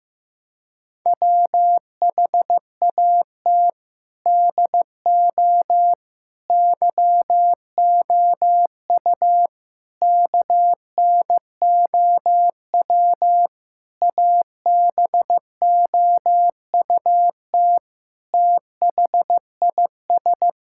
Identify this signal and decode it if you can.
Morse code, heard as a keyed tone, about 16 wpm, 700 Hz. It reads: WHAT DO YOU KNOW ABOUT THIS
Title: Sentences from Top 1000 Words - Review Part 6 - 15wpm